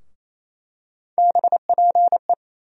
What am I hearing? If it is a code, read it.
Morse code, heard as a keyed tone, about 28 wpm, 700 Hz. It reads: BPE
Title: USA Call Sign Postfixes - 3 Letters 28wpm